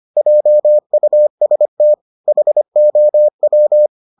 Morse code, heard as a keyed tone, about 25 wpm, 600 Hz. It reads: JUST HOW